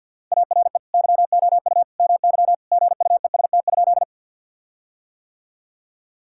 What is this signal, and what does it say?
Morse code, heard as a keyed tone, about 50 wpm, 700 Hz. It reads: ARE YOU MY GUEST?
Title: Sentences from Top 1600 Words 50wpm